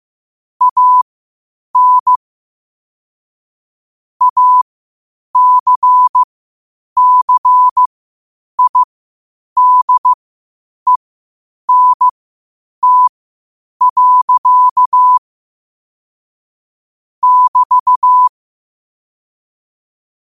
Morse code, heard as a keyed tone, about 15 wpm, spaced as for 8 wpm, 1000 Hz. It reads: AN ACCIDENT. <BT>
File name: qotdfiles / 8wpm